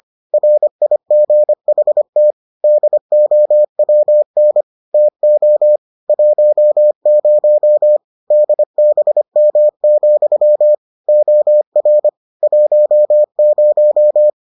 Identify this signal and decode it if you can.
Morse code, heard as a keyed tone, about 25 wpm, 600 Hz. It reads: RIGHT DOWN TO 10 DBM, OR 10